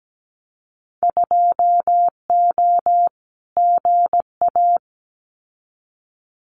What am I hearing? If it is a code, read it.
Morse code, heard as a keyed tone, about 17 wpm, 700 Hz. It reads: 2O GA
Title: Winter Field Day Exchanges Encore 17wpm